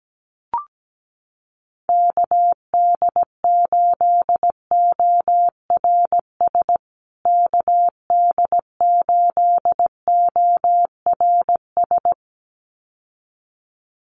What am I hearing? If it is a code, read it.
Morse code, heard as a keyed tone, about 17 wpm, 700 Hz. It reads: KD8ORS KD8ORS
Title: USA Call Signs Repeated 2x 17wpm